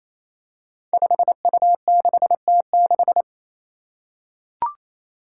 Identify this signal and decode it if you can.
Morse code, heard as a keyed tone, about 28 wpm, 700 Hz. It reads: 5U6T6
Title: Five Challenging Letter-Number Combinations 28wpm